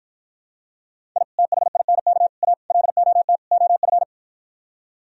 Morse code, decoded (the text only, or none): I THINK A LOT OF